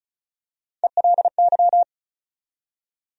Morse code, heard as a keyed tone, about 36 wpm, 700 Hz. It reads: ELY
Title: USA Call Sign Postfixes - 3 Letters 35wpm